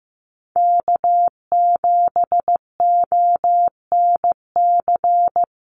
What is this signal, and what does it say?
Morse code, heard as a keyed tone, about 15 wpm, 700 Hz. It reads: K7ONC
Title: USA Call Signs 15wpm